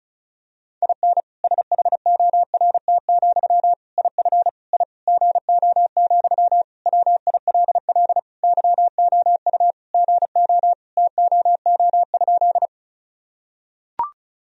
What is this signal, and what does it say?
Morse code, heard as a keyed tone, about 36 wpm, 700 Hz. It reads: IN SHORT, IF I GO, WILL YOU GO TOO?
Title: Sentences from Top 1000 Words - Review Part 2 - 35wpm